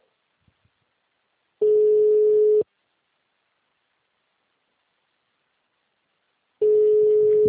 Ringback tone (Europe, 425 Hz).